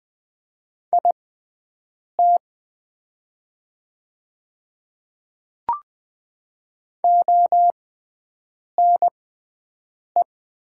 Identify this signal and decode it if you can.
Morse code, heard as a keyed tone, about 20 wpm, spaced as for 7 wpm, 690 Hz. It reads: IT ONE
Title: 6x Character Spacing - Top 200 Words - 20wpm